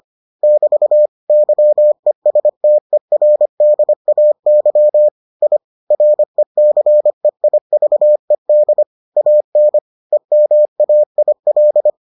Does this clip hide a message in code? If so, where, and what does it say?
Yes: Morse code from 0.4 to 11.9 s, heard as a keyed tone, about 25 wpm, 600 Hz. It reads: <BT> YESTERDAY I RECEIVED AN EMAIL